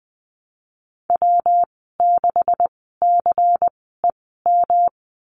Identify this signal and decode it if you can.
Morse code, heard as a keyed tone, about 20 wpm, spaced as for 15 wpm, 700 Hz. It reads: W6CEM
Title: 128 Lesson 21g - Number 6 - Call Signs 20wpm